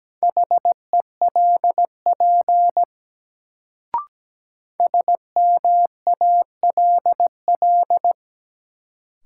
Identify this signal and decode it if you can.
Morse code, heard as a keyed tone, about 17 wpm, 700 Hz. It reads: HELP SMALL
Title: Sentences from Top 200 Words - New Word Drill - 17wpm